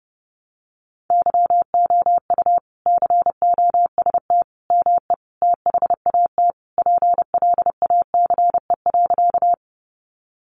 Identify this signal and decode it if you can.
Morse code, heard as a keyed tone, about 30 wpm, 700 Hz. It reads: YOU COST ME THAT PLACE.
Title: Sentences from Top 800 Words 30wpm